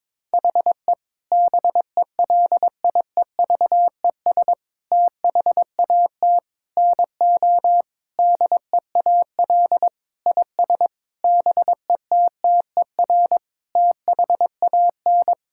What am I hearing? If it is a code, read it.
Morse code, heard as a keyed tone, about 22 wpm, 700 Hz. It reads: HE BELIEVES THAT NO DEAL IS BETTER THAN
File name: Sentences with 9-Word Phrases - 22wpm